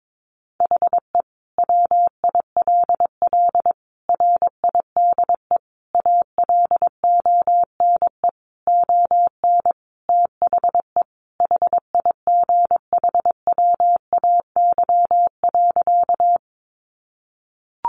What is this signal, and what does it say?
Morse code, heard as a keyed tone, about 22 wpm, 700 Hz. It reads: HE WILL RIDE ALONE ON THE HIGHWAY.
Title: Sentences from Top 1400 Words 22wpm